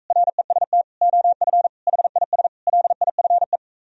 Morse code, heard as a keyed tone, about 45 wpm, 700 Hz. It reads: REST OF HIS LIFE